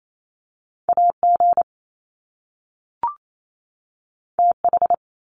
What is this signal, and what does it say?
Morse code, heard as a keyed tone, about 28 wpm, 700 Hz. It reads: AG TH
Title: Top Two Letter Combinations in Words 28wpm